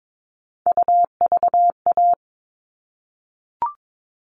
Morse code, heard as a keyed tone, about 22 wpm, 700 Hz. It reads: UVA
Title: All Cut Numbers - 3-Digit 22wpm